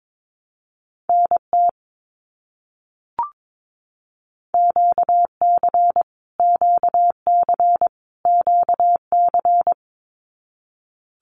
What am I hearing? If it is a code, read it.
Morse code, heard as a keyed tone, about 22 wpm, 700 Hz. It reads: NT QC QC QC
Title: Canadian Provinces and Territories - Repeated 3x - 22wpm